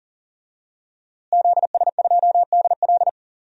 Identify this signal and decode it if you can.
Morse code, heard as a keyed tone, about 40 wpm, 700 Hz. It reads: ZS2DL
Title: DXCC-100 Call Signs - 5 Characters - Repeated 2x - 40wpm